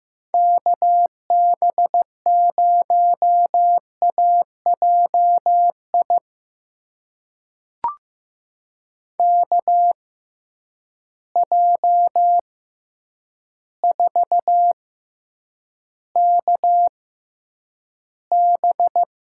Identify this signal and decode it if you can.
Morse code, heard as a keyed tone, about 15 wpm, spaced as for 8 wpm, 700 Hz. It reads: KB0AJI KJ4KB
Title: USA Call Signs - 6 Characters - 6x Character Spacing - 15wpm